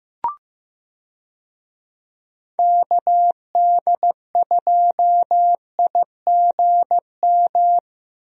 Morse code, heard as a keyed tone, about 15 wpm, 700 Hz. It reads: KD2IGM
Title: USA Call Signs - 6 Characters - Repeated 2x - 15wpm